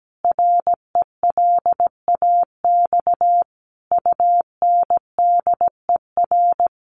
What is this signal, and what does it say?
Morse code, heard as a keyed tone, about 17 wpm, 700 Hz. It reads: RELAX UNDER
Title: Sentences from Top 1400 Words 17wpm